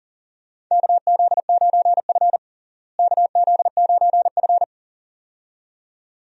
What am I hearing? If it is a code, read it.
Morse code, heard as a keyed tone, about 40 wpm, 700 Hz. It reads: KZ9F KZ9F
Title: USA Call Signs - 4 Characters - Repeated 2x - 40wpm